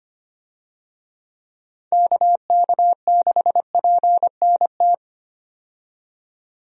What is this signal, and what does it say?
Morse code, heard as a keyed tone, about 25 wpm, 700 Hz. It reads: KK6PNT
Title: USA Call Signs - 6 Characters - 25wpm